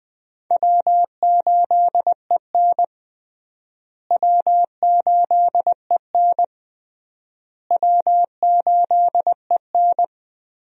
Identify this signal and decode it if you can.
Morse code, heard as a keyed tone, about 21 wpm, 700 Hz. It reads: W8EN W8EN W8EN
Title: USA Call Signs - 4 Characters - Repeated 3x - 20wpm